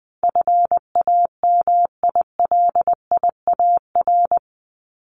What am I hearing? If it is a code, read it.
Morse code, heard as a keyed tone, about 20 wpm, 700 Hz. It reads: FAMILIAR
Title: Top 751-1000 Words 20wpm